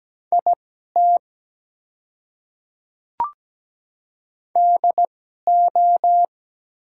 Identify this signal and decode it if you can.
Morse code, heard as a keyed tone, about 17 wpm, spaced as for 12 wpm, 700 Hz. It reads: IT DO
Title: 2x Character Spacing - Top 100 Words - 17wpm